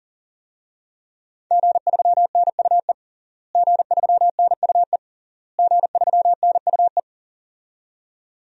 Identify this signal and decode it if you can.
Morse code, heard as a keyed tone, about 40 wpm, 700 Hz. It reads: G3NUE G3NUE G3NUE
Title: DXCC-100 Call Signs - 5 Characters - Repeated 3x - 40wpm